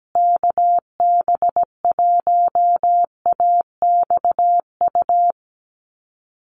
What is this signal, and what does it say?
Morse code, heard as a keyed tone, about 17 wpm, 700 Hz. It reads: KB1AXU
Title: USA Call Signs Repeated 3x 17wpm